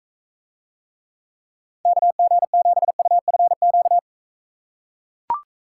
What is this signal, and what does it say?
Morse code, heard as a keyed tone, about 44 wpm, 700 Hz. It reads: KG7UFQ